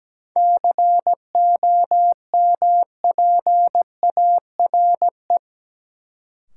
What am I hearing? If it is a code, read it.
Morse code, heard as a keyed tone, about 17 wpm, 700 Hz. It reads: COMPARE
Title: Sentences from Top 1300 Words - New Word Drill Alt - 17wpm